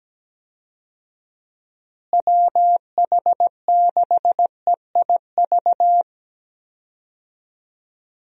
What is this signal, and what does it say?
Morse code, heard as a keyed tone, about 17 wpm, 700 Hz. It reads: WH6EIV
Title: USA Call Signs - 6 Characters - Repeated 3x - 17wpm